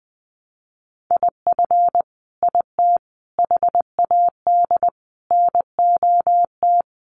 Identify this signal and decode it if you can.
Morse code, heard as a keyed tone, about 20 wpm, 700 Hz. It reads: IF IT HAD NOT
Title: Sets of 5 Words Encore 20wpm